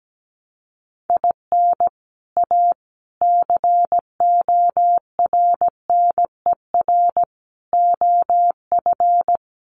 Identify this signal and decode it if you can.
Morse code, heard as a keyed tone, about 17 wpm, 700 Hz. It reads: IN A CORNER OF
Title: Sets of 5 Words Encore2 17wpm